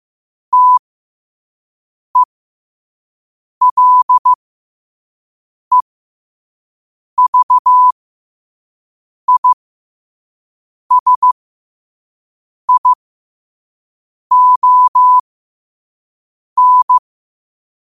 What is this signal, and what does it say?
Morse code, heard as a keyed tone, about 15 wpm, spaced as for 5 wpm, 1000 Hz. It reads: TELEVISION
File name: qotdfiles / 5wpm